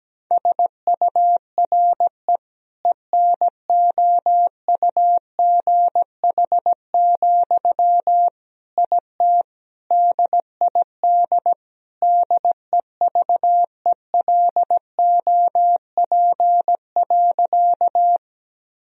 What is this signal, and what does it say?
Morse code, heard as a keyed tone, about 17 wpm, 700 Hz. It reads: SURE ENOUGH, IT DID DEVELOP.
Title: Sentences from Top 1000 Words - Review Part 2 - 17wpm